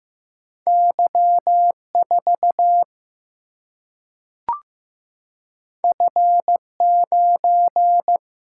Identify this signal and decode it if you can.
Morse code, heard as a keyed tone, about 15 wpm, 700 Hz. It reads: Y 4 F 9
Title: Single-Letter Single-Number 15wpm